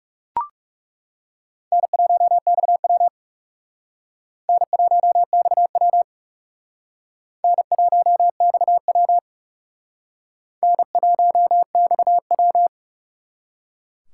Morse code, heard as a keyed tone, about 37 wpm, 700 Hz. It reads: N1XW N1XW N1XW N1XW
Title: Speed Racing - USA Call Signs 30wpm